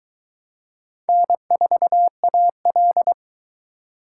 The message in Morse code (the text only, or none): N4AL